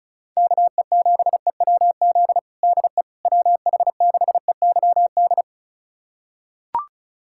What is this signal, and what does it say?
Morse code, heard as a keyed tone, about 34 wpm, 700 Hz. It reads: KE7EWZ DE WH6EYD
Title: USA Callsign DE Callsign 35wpm